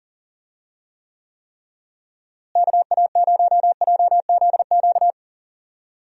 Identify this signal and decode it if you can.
Morse code, heard as a keyed tone, about 40 wpm, 700 Hz. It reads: KA0JZQ
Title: USA Call Signs - 6 Characters - Repeated 3x - 40wpm